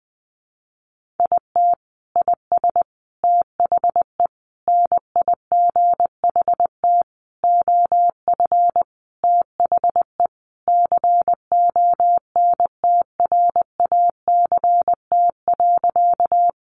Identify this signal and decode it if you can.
Morse code, heard as a keyed tone, about 19 wpm, 700 Hz. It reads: IT IS THE NIGHT OF THE CONTRACT.
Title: Sentences from Top 1400 Words 20wpm